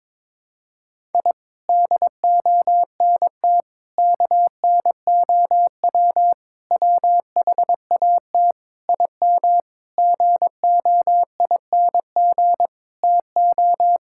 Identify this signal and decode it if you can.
Morse code, heard as a keyed tone, about 22 wpm, 700 Hz. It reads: I DONT KNOW WHAT IM GOING TO
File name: practice / 22wpm / Sentences with 8-Word Phrases - 22wpm